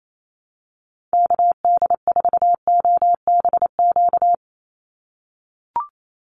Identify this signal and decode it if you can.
Morse code, heard as a keyed tone, about 28 wpm, 700 Hz. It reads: KD4OBQ
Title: USA Call Signs 28wpm